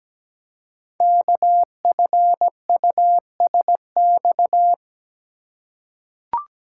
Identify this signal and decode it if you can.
Morse code, heard as a keyed tone, about 17 wpm, 700 Hz. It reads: KFUSX
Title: Code Groups - 5 Letters - 17wpm